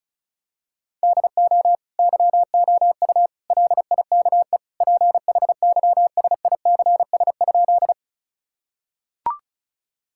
Morse code, heard as a keyed tone, about 36 wpm, 700 Hz. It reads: DO YOU LIKE PHYSICS?
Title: Sentences from Top 1000 Words - Review Part 5 - 35wpm